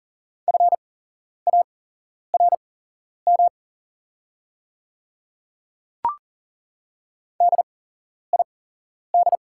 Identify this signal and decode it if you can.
Morse code, heard as a keyed tone, about 40 wpm, spaced as for 11 wpm, 690 Hz. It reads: FARM DID
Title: 8x Character Spacing - Top 500 Words - 40wpm